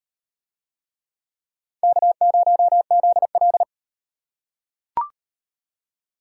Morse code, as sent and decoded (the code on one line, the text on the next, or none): -.- ----- --.. .-..
K0ZL